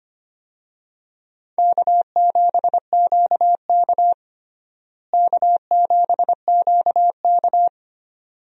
Morse code, heard as a keyed tone, about 25 wpm, 700 Hz. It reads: K7QK K7QK
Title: USA Call Signs - 4 Characters - Repeated 2x - 25wpm